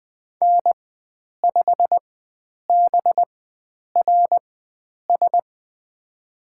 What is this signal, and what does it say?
Morse code, heard as a keyed tone, about 20 wpm, spaced as for 9 wpm, 700 Hz. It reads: N5BRS
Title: USA Call Signs - 4x Character Spacing - 20wpm